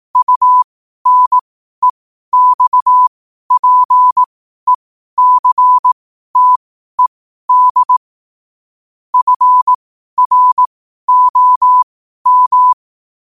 Morse code, heard as a keyed tone, about 18 wpm, spaced as for 12 wpm, 1000 Hz. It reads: UNEXPECTED FROM
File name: qotdfiles / 12wpm